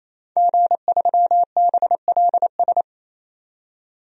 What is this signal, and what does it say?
Morse code, heard as a keyed tone, about 28 wpm, 700 Hz. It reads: G3BLS
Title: DXCC-100 Call Signs - 5 Characters - 28wpm